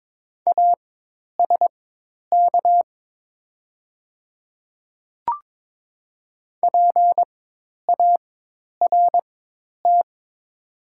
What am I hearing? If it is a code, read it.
Morse code, heard as a keyed tone, about 23 wpm, spaced as for 10 wpm, 700 Hz. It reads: ASK PART
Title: 4x Character Spacing - Top 200 Words - 22wpm